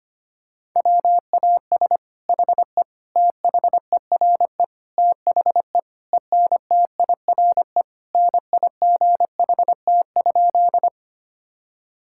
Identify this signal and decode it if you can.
Morse code, heard as a keyed tone, about 25 wpm, 700 Hz. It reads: WAS HE THERE THE ENTIRE NIGHT?
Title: Sentences from Top 1000 Words - Review Part 5 - 25wpm